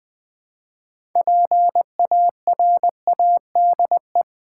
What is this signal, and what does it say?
Morse code, heard as a keyed tone, about 20 wpm, 700 Hz. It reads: PARADE
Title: Build A Word - 2 Letters At A Time - 6-Letter Words - 20wpm